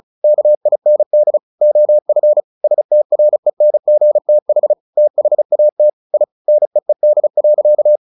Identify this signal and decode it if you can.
Morse code, heard as a keyed tone, about 34 wpm, 600 Hz. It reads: KIND OF STRENGTH THAT I NEED.
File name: the old man and the sea 2 35wpm